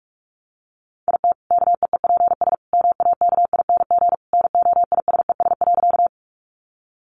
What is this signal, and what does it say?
Morse code, heard as a keyed tone, about 44 wpm, 700 Hz. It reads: IT KEEPS MAKING NOISES.
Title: Additional Easy Phrases 45wpm